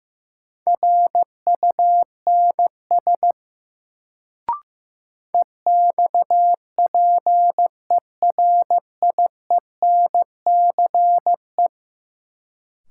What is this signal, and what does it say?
Morse code, heard as a keyed tone, about 15 wpm, 700 Hz. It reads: RUNS EXPERIENCE
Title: Sentences from Top 600 Words - New Word Drill - 15wpm